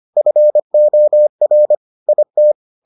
Morse code, heard as a keyed tone, about 25 wpm, 600 Hz. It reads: FOR IT